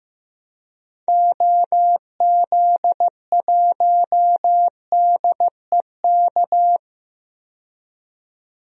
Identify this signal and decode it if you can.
Morse code, heard as a keyed tone, about 15 wpm, 700 Hz. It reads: OZ1DEK
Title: DXCC-100 Call Signs - 6 Characters - 15wpm